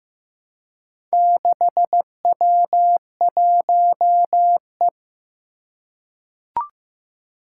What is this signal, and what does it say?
Morse code, heard as a keyed tone, about 15 wpm, 700 Hz. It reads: 6W1E
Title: Four Challenging Letter-Number Combinations 15wpm